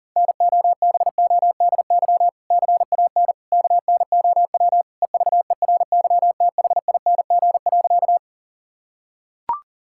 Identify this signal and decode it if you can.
Morse code, heard as a keyed tone, about 40 wpm, 700 Hz. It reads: NOBODY CAN KNOW EVERYTHING.
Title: Sentences from Top 1600 Words 40wpm